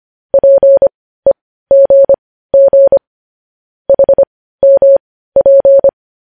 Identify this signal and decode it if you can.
Morse code, heard as a keyed tone, about 25 wpm, spaced as for 15 wpm, 550 Hz. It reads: PEGG HMP
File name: podcast / News headlines-1x-25x15